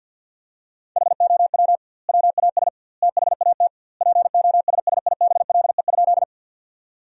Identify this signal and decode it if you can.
Morse code, heard as a keyed tone, about 50 wpm, 700 Hz. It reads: HOW WAS THAT POSSIBLE?
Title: 2x Word-Spacing - Sentences from Top 500 Words - 50wpm